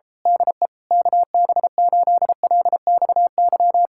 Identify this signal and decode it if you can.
Morse code, heard as a keyed tone, about 27 wpm, 700 Hz. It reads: DE KB8LXY
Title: Speed-Racing Common QSO Phrases - 28wpm